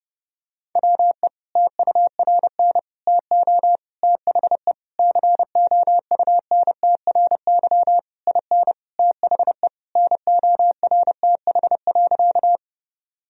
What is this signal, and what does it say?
Morse code, heard as a keyed tone, about 30 wpm, 700 Hz. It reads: WE TURN TO THE COUNTRY IN THE NORTH.
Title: Sentences from Top 1000 Words - Review Part 1 - 30wpm